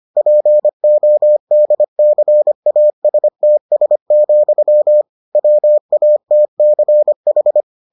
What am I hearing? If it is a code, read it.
Morse code, heard as a keyed tone, about 25 wpm, 600 Hz. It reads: PODCASTS, WATCH